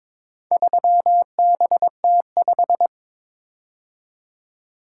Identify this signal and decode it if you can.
Morse code, heard as a keyed tone, about 22 wpm, 700 Hz. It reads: 3BT5
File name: practice / 22wpm / Four Letter-Number Combinations - 22wpm